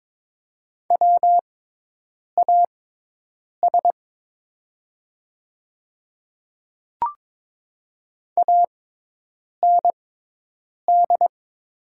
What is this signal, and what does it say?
Morse code, heard as a keyed tone, about 23 wpm, spaced as for 8 wpm, 700 Hz. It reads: WAS AND
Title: 6x Character Spacing - Top 100 Words - 22wpm